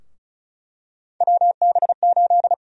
Morse code, heard as a keyed tone, about 34 wpm, 700 Hz. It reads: WB8
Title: USA Call Sign Prefixes - 35wpm